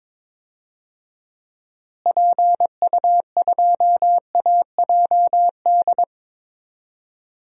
Morse code, heard as a keyed tone, about 22 wpm, 700 Hz. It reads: PU2AJD
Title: DXCC-100 Call Signs - 6 Characters - Repeated 3x - 22wpm